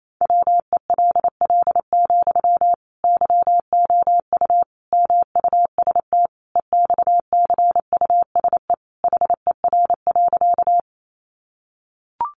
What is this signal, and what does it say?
Morse code, heard as a keyed tone, about 28 wpm, 700 Hz. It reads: WELL, YOU MUST EXCUSE HER.
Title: Sentences from Top 700 Words 28wpm